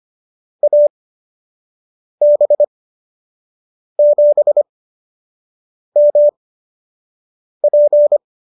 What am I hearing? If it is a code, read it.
Morse code, heard as a keyed tone, about 25 wpm, spaced as for 6 wpm, 590 Hz. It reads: AB7MP